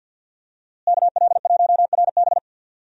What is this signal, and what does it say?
Morse code, heard as a keyed tone, about 52 wpm, 700 Hz. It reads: KL1RB
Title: USA Call Signs - 5 Characters - Repeated 3x - 50wpm